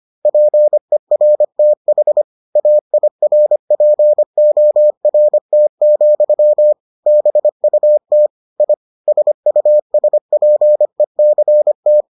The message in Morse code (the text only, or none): PERTH AIRPORT, BUT I SUSPECT